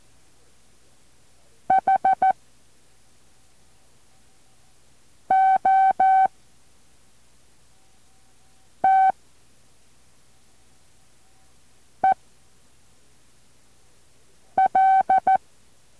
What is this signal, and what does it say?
Morse code, heard as a keyed tone, about 14 wpm, spaced as for 3 wpm, 760 Hz. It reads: HOTEL
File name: intl morse code2